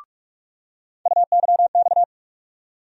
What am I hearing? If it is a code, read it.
Morse code, heard as a keyed tone, about 46 wpm, 700 Hz. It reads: UYX
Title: USA Call Sign Postfixes - 3 Letters 45wpm